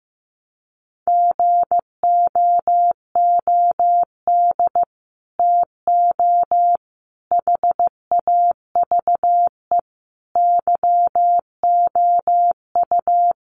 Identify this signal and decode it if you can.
Morse code, heard as a keyed tone, about 15 wpm, 700 Hz. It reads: GOOD TO HAVE YOU
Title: Sets of 5 Words Encore2 15wpm